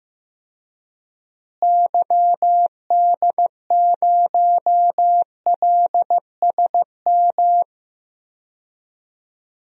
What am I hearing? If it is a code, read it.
Morse code, heard as a keyed tone, about 15 wpm, 700 Hz. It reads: YD0LSM
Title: DXCC-100 Call Signs - 6 Characters - Repeated 3x - 15wpm